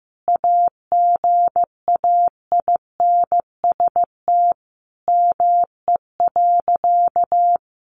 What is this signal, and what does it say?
Morse code, heard as a keyed tone, about 15 wpm, 700 Hz. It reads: AGAINST ME.
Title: Sentences from Top 1400 Words 15wpm